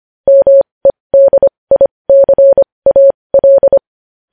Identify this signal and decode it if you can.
Morse code, heard as a keyed tone, about 25 wpm, spaced as for 20 wpm, 550 Hz. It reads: MEDICAL